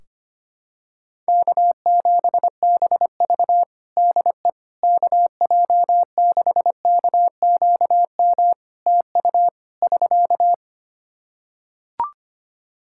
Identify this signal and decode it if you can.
Morse code, heard as a keyed tone, about 25 wpm, 700 Hz. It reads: K7BV DE KJ6KQM TU <SK>
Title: Practice QSO - Exchange 4 - Ending - 25wpm